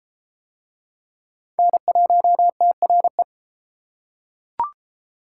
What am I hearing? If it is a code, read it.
Morse code, heard as a keyed tone, about 33 wpm, 700 Hz. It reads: N1TRE